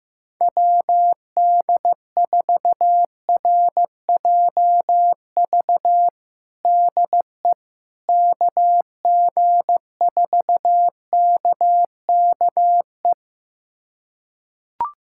Morse code, heard as a keyed tone, about 15 wpm, 700 Hz. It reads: WD4RJV DE KG4KKE
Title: USA Callsign DE Callsign - 6 Characters - 15wpm